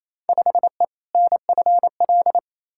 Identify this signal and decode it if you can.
Morse code, heard as a keyed tone, about 28 wpm, 700 Hz. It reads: 5E NFL
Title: Field Day Exchanges Encore - 28wpm